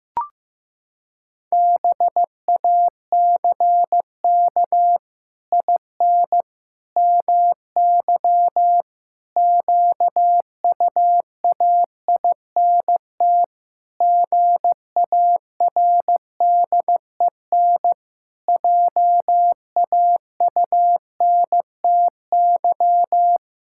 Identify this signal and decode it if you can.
Morse code, heard as a keyed tone, about 15 wpm, 700 Hz. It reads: BACK IN MY QUAINT GARDEN JAUNTY
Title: Pangram Challenge v2 - 15wpm